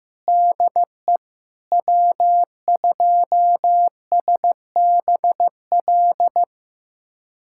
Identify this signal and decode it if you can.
Morse code, heard as a keyed tone, about 16 wpm, 700 Hz. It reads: DE W2SBL
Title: USA Callsign DE Callsign - 5 Characters - 15wpm